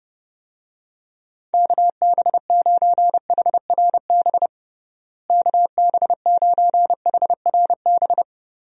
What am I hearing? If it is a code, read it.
Morse code, heard as a keyed tone, about 30 wpm, 700 Hz. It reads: KB9HRB KB9HRB
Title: USA Call Signs - 6 Characters - Repeated 2x - 30wpm